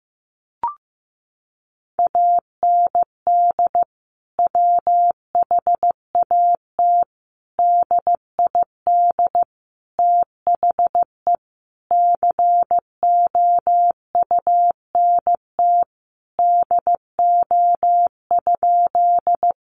Morse code, heard as a keyed tone, about 15 wpm, 700 Hz. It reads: AND WHAT DID THE COUNT DO?
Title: Sentences from Top 1500 Words 15wpm